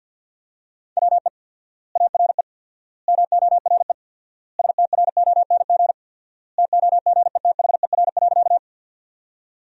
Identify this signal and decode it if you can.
Morse code, heard as a keyed tone, about 48 wpm, 700 Hz. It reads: WE ARE MORE STRONG TOGETHER.
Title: 4x Word-Spacing - Sentences from Top 500 Words - 50wpm